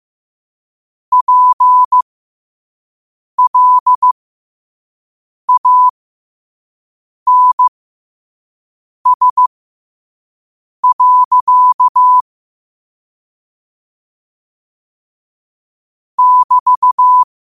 Morse code, heard as a keyed tone, about 15 wpm, spaced as for 5 wpm, 1000 Hz. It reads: PLANS. <BT>
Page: New quotes every day in morse code at 5 Words per minute.